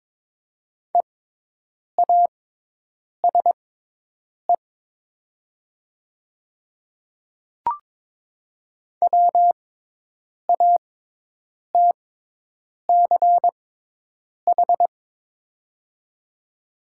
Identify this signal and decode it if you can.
Morse code, heard as a keyed tone, about 22 wpm, spaced as for 8 wpm, 700 Hz. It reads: EASE WATCH
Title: 6x Character Spacing - Top 300 Words - 22wpm